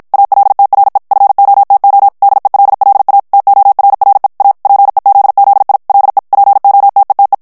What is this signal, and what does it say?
Morse code, heard as a keyed tone, about 53 wpm, 800 Hz. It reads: ALTRE FOTO DELLA TORRE A PEZZI LE POTETE
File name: ITALIEN texte clair 1 vitesse 50 mots minute